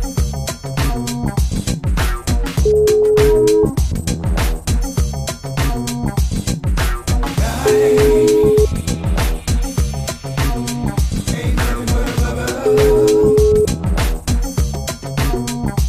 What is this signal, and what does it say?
Ringback tone (Europe, 425 Hz), with noise.